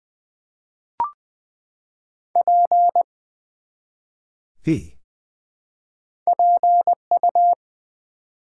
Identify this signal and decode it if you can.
Morse code, heard as a keyed tone, about 20 wpm, 700 Hz. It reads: P PU
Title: Build A Word - 1 Letter At A Time - 3-Letter Words - 20wpm